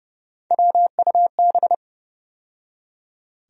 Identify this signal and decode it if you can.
Morse code, heard as a keyed tone, about 30 wpm, 700 Hz. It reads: WUB
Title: USA Call Sign Postfixes - 3 Letters 30wpm